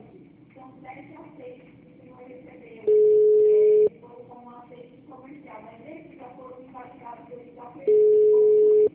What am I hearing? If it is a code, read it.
Ringback tone (Europe, 425 Hz), with speech.
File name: out- -2003- - - .150